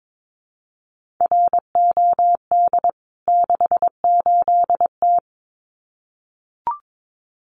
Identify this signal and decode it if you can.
Morse code, heard as a keyed tone, about 22 wpm, 700 Hz. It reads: ROD 68T
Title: CWT Exchange 22wpm